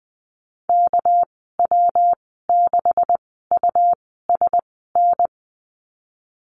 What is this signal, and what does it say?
Morse code, heard as a keyed tone, about 20 wpm, spaced as for 15 wpm, 700 Hz. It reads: KW6USN
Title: 207 Lesson 33g - Letter j - Call Signs 20wpm